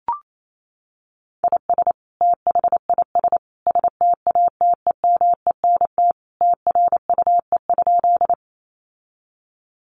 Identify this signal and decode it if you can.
Morse code, heard as a keyed tone, about 28 wpm, 700 Hz. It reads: IS THIS STATEMENT TRUE?
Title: Sentences from Top 1200 Words 28wpm